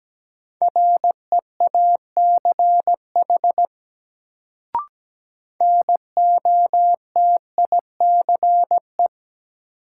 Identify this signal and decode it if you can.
Morse code, heard as a keyed tone, about 17 wpm, 700 Hz. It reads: REACH NOTICE
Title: Sentences from Top 400 Words - New Word Drill - 17wpm